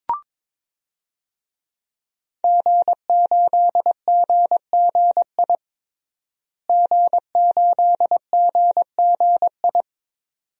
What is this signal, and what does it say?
Morse code, heard as a keyed tone, about 22 wpm, 700 Hz. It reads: G8GGI G8GGI
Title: DXCC-100 Call Signs - 5 Characters - Repeated 2x - 22wpm